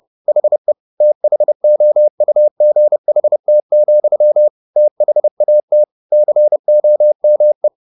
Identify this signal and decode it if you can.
Morse code, heard as a keyed tone, about 30 wpm, 600 Hz. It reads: HE THOUGHT, THAT COME
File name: the old man and the sea / the old man and the sea 1 30wpm